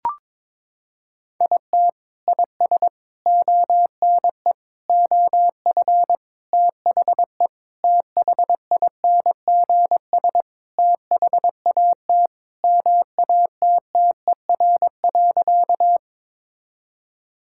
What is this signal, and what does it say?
Morse code, heard as a keyed tone, about 22 wpm, 700 Hz. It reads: IT IS ONE OF THE THINGS THAT MATTER.
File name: practice / 22wpm / Sentences with 4-Word Phrases - 22wpm